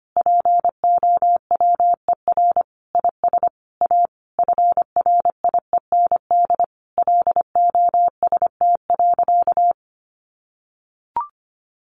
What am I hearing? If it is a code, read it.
Morse code, heard as a keyed tone, about 25 wpm, 700 Hz. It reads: POWER IS A FRIEND LOST.
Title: Sentences from Top 1400 Words 25wpm